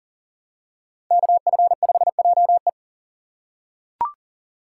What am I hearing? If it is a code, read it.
Morse code, heard as a keyed tone, about 37 wpm, 700 Hz. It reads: KF5JE
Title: USA Call Signs - 6x Character Spacing - 40wpm